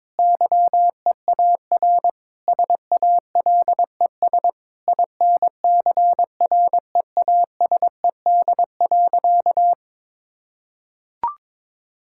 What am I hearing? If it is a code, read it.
Morse code, heard as a keyed tone, about 22 wpm, 700 Hz. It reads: YEAR SALES INCREASED.
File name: Sentences with 7-Word Phrases - 22wpm